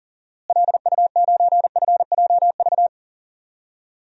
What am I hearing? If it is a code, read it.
Morse code, heard as a keyed tone, about 40 wpm, 700 Hz. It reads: LU9FJV
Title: DXCC-100 Call Signs - 6 Characters - Repeated 3x - 40wpm